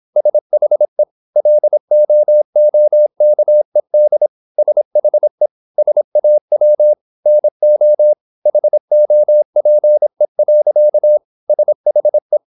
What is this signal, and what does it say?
Morse code, heard as a keyed tone, about 26 wpm, 600 Hz. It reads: SHE LOOKED SHE SAW NO HOPE. SHE
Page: Never 26wpm | CW med Gnister